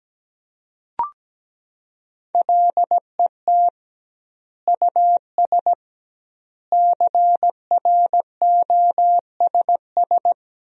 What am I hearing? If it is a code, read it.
Morse code, heard as a keyed tone, about 17 wpm, 700 Hz. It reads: LET US CROSS
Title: 2x Word-Spacing - Sentences from Top 300 Words - 17wpm